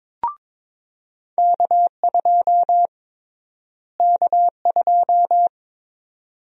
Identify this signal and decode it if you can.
Morse code, heard as a keyed tone, about 22 wpm, 700 Hz. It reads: K2 K2
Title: USA Call Sign Prefixes - Repeated 2x - 22wpm